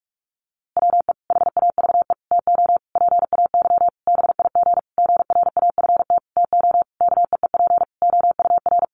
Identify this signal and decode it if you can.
Morse code, heard as a keyed tone, about 44 wpm, 700 Hz. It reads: WE HAVE TO PAY BIG GRAFT TO KEEP OUR
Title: A Sound Of Thunder 45wpm